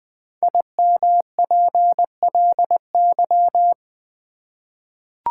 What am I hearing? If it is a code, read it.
Morse code, heard as a keyed tone, about 20 wpm, 700 Hz. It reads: IMPLY
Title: 5 Letter Words - First 3 Letters Given - 20wpm